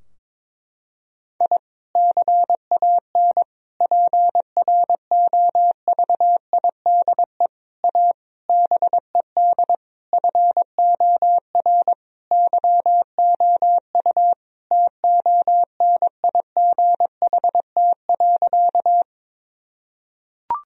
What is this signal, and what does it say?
Morse code, heard as a keyed tone, about 22 wpm, 700 Hz. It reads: I CAN PROVIDE A BED FOR YOU TONIGHT.
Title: Sentences from Top 1000 Words - Review Part 3 - 22wpm